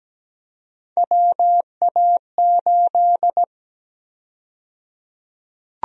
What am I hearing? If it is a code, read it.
Morse code, heard as a keyed tone, about 17 wpm, 700 Hz. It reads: WA8
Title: USA Call Sign Prefixes - 17wpm